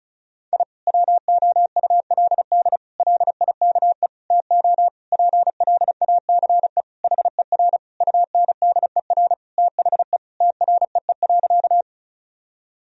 Morse code, heard as a keyed tone, about 34 wpm, 700 Hz. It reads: I WOULD LIKE TO PLACE HER UNDER THE TREE.
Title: Sentences from Top 1000 Words - Review Part 1 - 35wpm